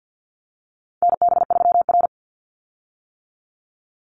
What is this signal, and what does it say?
Morse code, heard as a keyed tone, about 50 wpm, 700 Hz. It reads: NB3R
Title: Call Signs - 50wpm